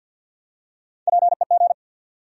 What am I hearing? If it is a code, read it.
Morse code, heard as a keyed tone, about 46 wpm, 700 Hz. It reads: PEG
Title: USA Common Names - Max 3 Letters - 50wpm